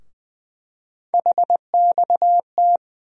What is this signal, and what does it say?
Morse code, heard as a keyed tone, about 20 wpm, 700 Hz. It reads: HXT
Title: USA Call Sign Postfixes - 3 Letters 20wpm